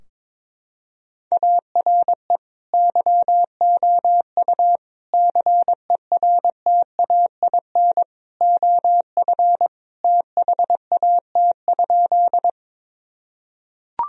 Morse code, heard as a keyed tone, about 22 wpm, 700 Hz. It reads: ARE YOU CERTAIN OF THAT?
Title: Sentences from Top 1000 Words - Review Part 2 - 22wpm